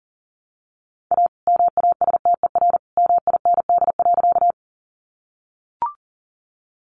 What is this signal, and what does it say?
Morse code, heard as a keyed tone, about 40 wpm, 700 Hz. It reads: A MASTER MIND.
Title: Sentences from Top 1300 Words 40wpm